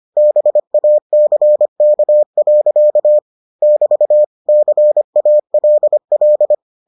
Morse code, heard as a keyed tone, about 25 wpm, 600 Hz. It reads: BACK. <BT> CALL